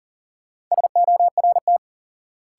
Morse code, heard as a keyed tone, about 40 wpm, 700 Hz. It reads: SORT
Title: Sentences from Top 1300 Words - New Word Drill Alt - 40wpm